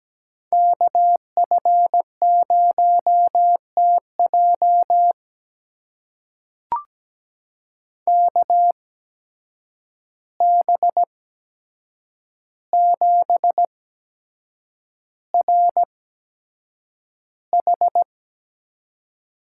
Morse code, heard as a keyed tone, about 17 wpm, spaced as for 7 wpm, 700 Hz. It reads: KF0TJ KB7RH
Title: USA Call Signs - 5 Characters - 8x Character Spacing - 17wpm